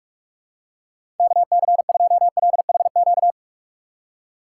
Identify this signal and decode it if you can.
Morse code, heard as a keyed tone, about 46 wpm, 700 Hz. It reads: KC2LHQ
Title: USA Call Signs - 6 Characters - 4x Character Spacing - 45wpm